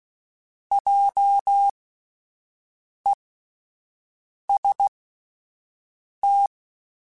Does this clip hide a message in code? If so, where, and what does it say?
Yes: Morse code from 0.7 to 6.5 s, heard as a keyed tone, about 16 wpm, spaced as for 5 wpm, 800 Hz. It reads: JEST